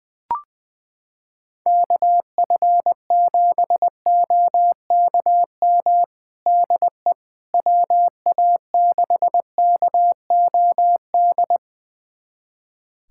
Morse code, heard as a keyed tone, about 20 wpm, 700 Hz. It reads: KF7OKM DE WA6KOD
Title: USA Callsign DE Callsign - 6 Characters - 20wpm